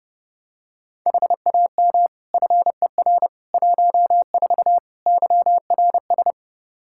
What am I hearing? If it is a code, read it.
Morse code, heard as a keyed tone, about 30 wpm, 700 Hz. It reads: HAM FER 14 YRS
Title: K1USN SST QSO Practice 30wpm